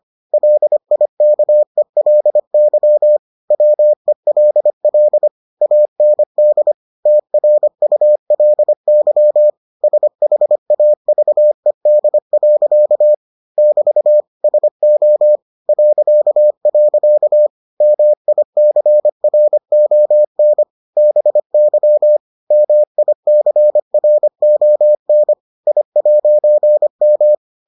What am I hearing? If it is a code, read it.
Morse code, heard as a keyed tone, about 25 wpm, 600 Hz. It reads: LIKELY WELL AND TRULY SHAVED. <BT> SO .. MICRON BY MICRON I'M